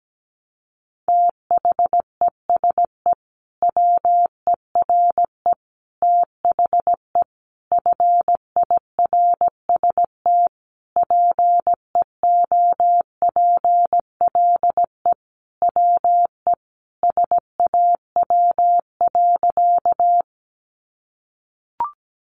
Morse code, heard as a keyed tone, about 17 wpm, 700 Hz. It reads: THESE WERE THE FIRST PEOPLE WE SAW.
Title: Sentences from Top 300 Words 17wpm